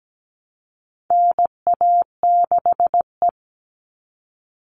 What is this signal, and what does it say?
Morse code, heard as a keyed tone, about 17 wpm, 700 Hz. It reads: NA6E
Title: All Cut Numbers - 4-Digit 17wpm